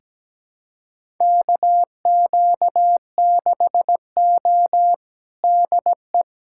Morse code, heard as a keyed tone, about 17 wpm, 700 Hz. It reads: KQ6O DE
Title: USA Callsign DE Callsign - 4 Characters - 17wpm